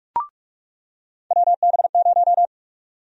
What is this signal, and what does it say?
Morse code, heard as a keyed tone, about 46 wpm, 700 Hz. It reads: WB0
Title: USA Call Sign Prefixes - 45wpm